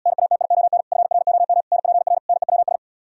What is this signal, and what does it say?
Morse code, heard as a keyed tone, about 77 wpm, 700 Hz. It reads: NATION FROM NPR NEWS